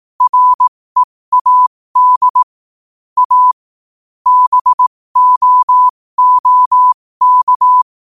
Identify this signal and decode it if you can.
Morse code, heard as a keyed tone, about 18 wpm, spaced as for 15 wpm, 1000 Hz. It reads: READ A BOOK
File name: qotdfiles / 15wpm